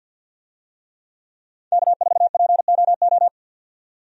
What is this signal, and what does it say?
Morse code, heard as a keyed tone, about 51 wpm, 700 Hz. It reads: K4POO
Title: USA Call Signs - 5 Characters - 50wpm